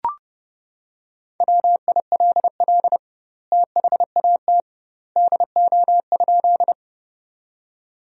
Morse code, heard as a keyed tone, about 30 wpm, 700 Hz. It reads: WILL THAT DO?
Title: 2x Word-Spacing - Sentences from Top 100 Words - 30wpm